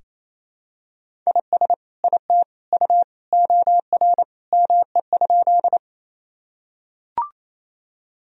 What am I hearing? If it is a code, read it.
Morse code, heard as a keyed tone, about 28 wpm, 700 Hz. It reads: IS IT U OR ME?
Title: Sentences from Top 200 Words and Abbreviations 28wpm